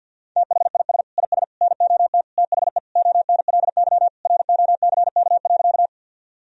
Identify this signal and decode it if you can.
Morse code, heard as a keyed tone, about 50 wpm, 700 Hz. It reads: THIS IS NOT THE ONLY ROCK.
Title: Sentences from Top 1000 Words - Review Part 2 - 50wpm